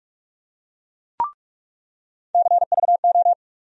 Morse code, heard as a keyed tone, about 45 wpm, 700 Hz. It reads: CVO
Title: USA Call Sign Postfixes - 3 Letters 45wpm